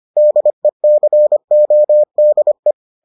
Morse code, heard as a keyed tone, about 25 wpm, 600 Hz. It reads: DECODE